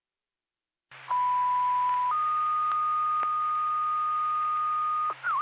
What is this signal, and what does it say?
Two-tone page, heard as a one-second tone, then a three-second tone, about 990 then 1230 Hz. It is heard from 1.1 to 5.1 s.